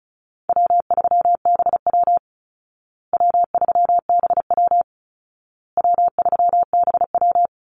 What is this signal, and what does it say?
Morse code, heard as a keyed tone, about 34 wpm, 700 Hz. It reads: W3BW W3BW W3BW
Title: Easy Call Signs Repeated 3x - 35wpm